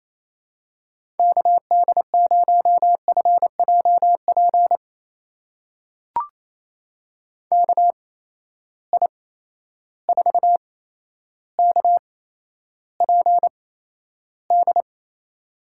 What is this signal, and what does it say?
Morse code, heard as a keyed tone, about 28 wpm, spaced as for 12 wpm, 700 Hz. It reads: KD0FJP KI4KPD
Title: USA Call Signs - 6 Characters - 8x Character Spacing - 28wpm